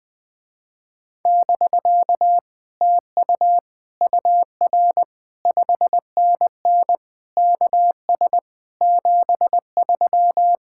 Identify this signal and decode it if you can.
Morse code, heard as a keyed tone, about 20 wpm, 700 Hz. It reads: <BK> TU UR 5NN KS 73
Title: Speed-Racing Common QSO Phrases Encore - 20wpm